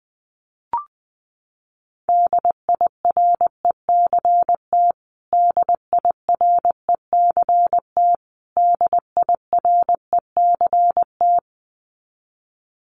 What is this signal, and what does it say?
Morse code, heard as a keyed tone, about 20 wpm, 700 Hz. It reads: DIRECT DIRECT DIRECT
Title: Top 500 Words Repeated 20wpm